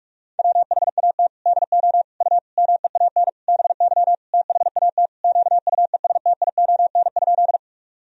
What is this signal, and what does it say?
Morse code, heard as a keyed tone, about 41 wpm, 700 Hz. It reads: WHAT DO U MEAN BY THAT QUESTION?
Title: Sentences from Top 400 Words and Abbreviations 45wpm